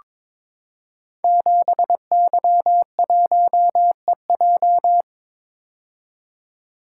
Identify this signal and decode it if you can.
Morse code, heard as a keyed tone, about 22 wpm, 700 Hz. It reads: 7Y1EJ
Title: Code Groups - 5 Letter:Numbers - 22wpm